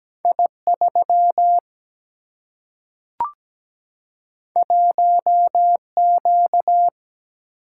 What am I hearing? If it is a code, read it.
Morse code, heard as a keyed tone, about 17 wpm, 700 Hz. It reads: I3 1Q
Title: Single-Letter Single-Number 17wpm